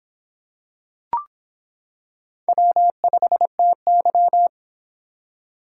Morse code, heard as a keyed tone, about 26 wpm, 700 Hz. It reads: W5TY